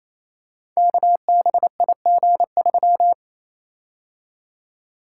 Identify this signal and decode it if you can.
Morse code, heard as a keyed tone, about 27 wpm, 700 Hz. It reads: KBIG3
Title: Code Groups - 5 Letter:Numbers - 28wpm